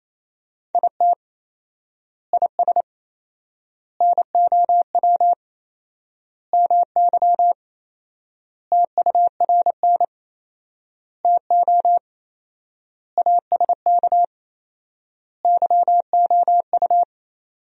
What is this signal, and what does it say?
Morse code, heard as a keyed tone, about 27 wpm, 700 Hz. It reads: IT IS NOW MY TURN TO ASK YOU
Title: 4x Word-Spacing - Sentences from Top 200 Words - 28wpm